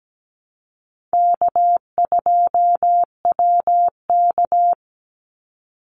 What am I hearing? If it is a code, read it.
Morse code, heard as a keyed tone, about 17 wpm, 700 Hz. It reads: K2WK
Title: Call Signs - 17wpm